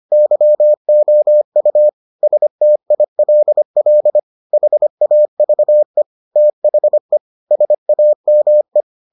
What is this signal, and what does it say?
Morse code, heard as a keyed tone, about 25 wpm, 600 Hz. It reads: YOU STILL HAVE THE SAME